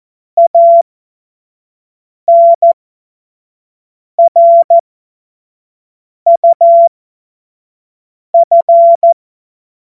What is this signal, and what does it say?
Morse code, heard as a keyed tone, about 14 wpm, spaced as for 5 wpm, 680 Hz. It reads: ANRUF
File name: Morsekurs / Witze / Witz 20 Februar 2024 25 F70